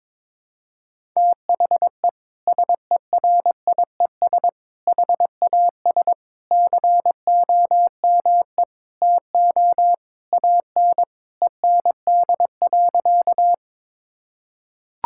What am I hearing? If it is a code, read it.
Morse code, heard as a keyed tone, about 22 wpm, 700 Hz. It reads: THE SERIES HAS COME TO AN END.
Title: Sentences from Top 1000 Words - Review Part 5 - 22wpm